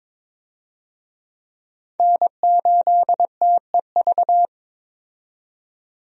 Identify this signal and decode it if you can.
Morse code, heard as a keyed tone, about 22 wpm, 700 Hz. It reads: N8TEV
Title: USA Call Signs - 5 Characters - 22wpm